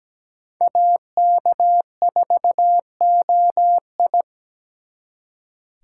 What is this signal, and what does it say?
Morse code, heard as a keyed tone, about 20 wpm, 700 Hz. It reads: AK4OI
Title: Speed Racing - USA Call Signs 17wpm